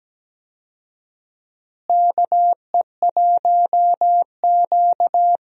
Morse code, heard as a keyed tone, about 17 wpm, 700 Hz. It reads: KE1Q
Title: USA Call Signs - 4 Characters - Repeated 2x - 17wpm